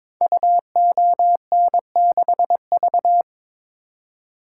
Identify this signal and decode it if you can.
Morse code, heard as a keyed tone, about 22 wpm, 700 Hz. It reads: UON6V
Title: Code Groups - 5 Letter:Numbers - 22wpm